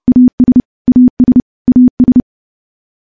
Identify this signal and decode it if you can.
Morse code, heard as a keyed tone, about 30 wpm, 250 Hz. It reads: AS AS AS
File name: morse / morse-training-30WPM